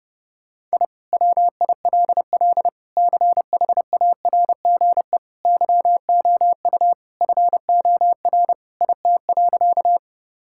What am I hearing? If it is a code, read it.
Morse code, heard as a keyed tone, about 30 wpm, 700 Hz. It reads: I WILL CHARGE YOU FOR IT.
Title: Sentences from Top 1000 Words - Review Part 6 - 30wpm